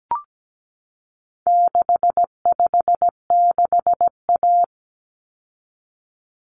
Morse code, heard as a keyed tone, about 17 wpm, 700 Hz. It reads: 656A
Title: Common Cut Numbers - 4-Digit 17wpm